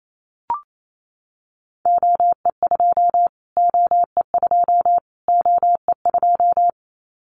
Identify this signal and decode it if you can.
Morse code, heard as a keyed tone, about 28 wpm, 700 Hz. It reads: OE2 OE2 OE2
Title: Callsign Prefixes for Top 25 DXCC - Repeated 3x - 28wpm